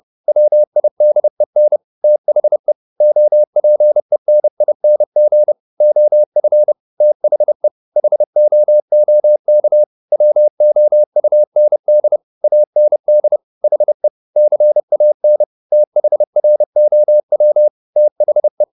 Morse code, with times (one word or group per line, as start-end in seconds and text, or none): 0.3-1.8 WIDEN
2.0-2.7 THE
3.0-5.5 OPENING
5.8-6.7 OF
7.0-7.7 THE
8.0-9.8 HOOK
10.1-12.2 WOUND
12.4-13.4 AND
13.6-14.1 HE
14.4-15.4 CAN
15.7-17.7 THROW
18.0-18.6 THE